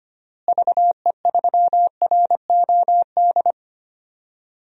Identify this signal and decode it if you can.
Morse code, heard as a keyed tone, about 25 wpm, 700 Hz. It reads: VE3ROD
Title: DXCC-100 Call Signs - 6 Characters - Repeated 3x - 25wpm